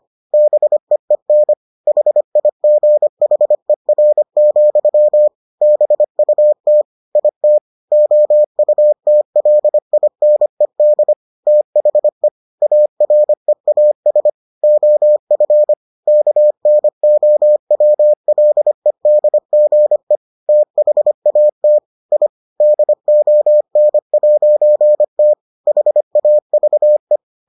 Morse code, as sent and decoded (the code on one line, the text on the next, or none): -... . -. / .... .. --. .... .-. --..-- / -... ..- - / .. - / --- ..- - .-.. .. -. -.. / - .... / .- .-. .- ... / --- ..-. / -.- -. --- .-- .-.. -.. --. / - .... .- - / .. / -.. --- -. .----. - / .... .- ...- .
BEEN HIGHER, BUT IT OUTLINED THE AREAS OF KNOWLEDGE THAT I DON'T HAVE